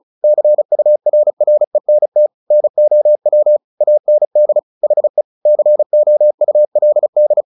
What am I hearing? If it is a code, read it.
Morse code, heard as a keyed tone, about 34 wpm, 600 Hz. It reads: CURRENT NOW AND HE COULD
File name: the old man and the sea 2 35wpm